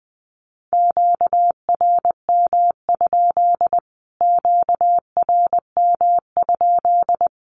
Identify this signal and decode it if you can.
Morse code, heard as a keyed tone, about 20 wpm, 700 Hz. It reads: QRM? QRM?
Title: Q-Codes Repeated 3x 20wpm